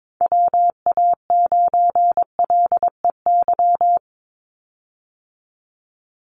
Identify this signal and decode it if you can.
Morse code, heard as a keyed tone, about 22 wpm, 700 Hz. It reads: WA9LEY
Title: Call Signs Repeated 2x - 22wpm